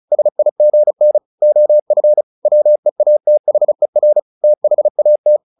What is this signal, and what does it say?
Morse code, heard as a keyed tone, about 34 wpm, 600 Hz. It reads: SIGN OF WEATHER THAT
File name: the old man and the sea / the old man and the sea 1 35wpm